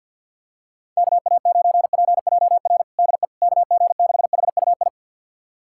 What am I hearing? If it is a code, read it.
Morse code, heard as a keyed tone, about 52 wpm, 700 Hz. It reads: KA9PJR DE KG6HUI
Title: USA Callsign DE Callsign 50wpm